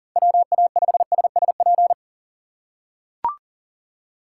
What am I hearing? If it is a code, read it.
Morse code, heard as a keyed tone, about 40 wpm, 700 Hz. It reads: WA5SSP
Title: USA Call Signs - 3x Character Spacing - 40wpm